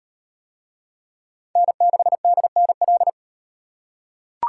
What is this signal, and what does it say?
Morse code, heard as a keyed tone, about 36 wpm, 700 Hz. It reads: N6DNL